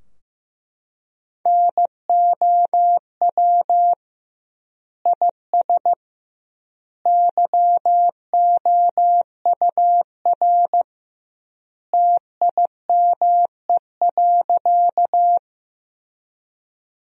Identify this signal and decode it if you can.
Morse code, heard as a keyed tone, about 16 wpm, 700 Hz. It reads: NOW IS YOUR TIME.
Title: 2x Word-Spacing - Sentences from Top 100 Words - 15wpm